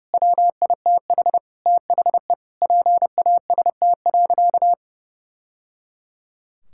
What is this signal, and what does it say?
Morse code, heard as a keyed tone, about 30 wpm, 700 Hz. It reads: WITH THE PAST.
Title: Sentences from Top 1000 Words - Review Part 6 - 30wpm